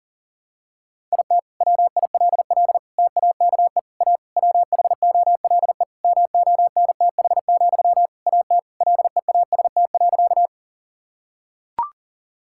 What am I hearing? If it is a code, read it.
Morse code, heard as a keyed tone, about 40 wpm, 700 Hz. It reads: IT WILL TAKE A WHOLE MONTH, AT LEAST.
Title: Sentences from Top 1000 Words - Review Part 5 - 40wpm